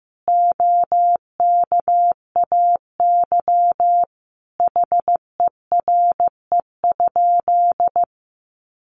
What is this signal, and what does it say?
Morse code, heard as a keyed tone, about 14 wpm, 700 Hz. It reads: OKAY HERE?
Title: A Sound Of Thunder 15wpm